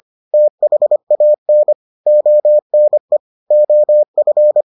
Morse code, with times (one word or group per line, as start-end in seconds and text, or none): 0.3-1.7 THAN
2.1-3.2 ONE
3.5-4.6 OF